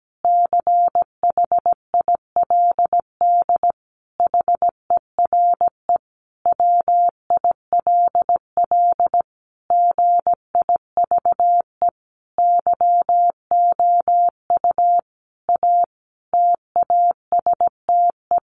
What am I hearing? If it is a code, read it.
Morse code, heard as a keyed tone, about 17 wpm, 700 Hz. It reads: CHILD HERE WILL GIVE YOU A TASTE
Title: Sentences from Top 1100 Words 17wpm